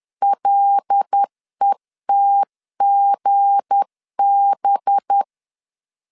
Morse code, heard as a keyed tone, about 10 wpm, 800 Hz. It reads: LETGB
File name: LEZIONI / 7 GF9Z / 7 G 10wpm 2501